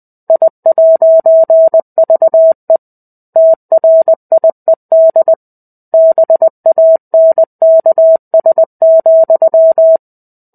Morse code, heard as a keyed tone, about 20 wpm, 650 Hz. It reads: I'VE TRIED BANKS,
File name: 05-Chapter-05 20 WPM